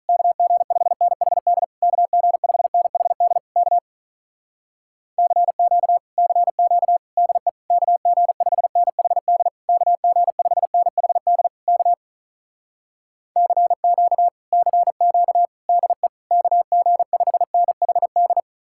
Morse code, heard as a keyed tone, about 46 wpm, 700 Hz. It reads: KG5NHD KG5NHD K CQ CQ DE KG5NHD KG5NHD K CQ CQ DE KG5NHD